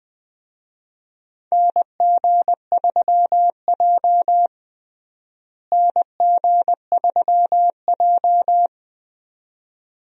Morse code, heard as a keyed tone, about 20 wpm, 700 Hz. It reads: NG3J NG3J
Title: USA Call Signs - 4 Characters - Repeated 2x - 20wpm